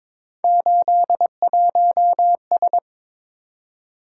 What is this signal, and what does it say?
Morse code, heard as a keyed tone, about 22 wpm, 700 Hz. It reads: 81S
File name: practice / 22wpm / Three Letter-Number Combinations - 22wpm